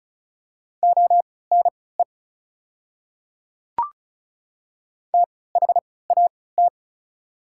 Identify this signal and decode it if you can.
Morse code, heard as a keyed tone, about 36 wpm, spaced as for 20 wpm, 690 Hz. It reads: ONE THAT
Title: 3x Character Spacing - Top 100 Words - 35wpm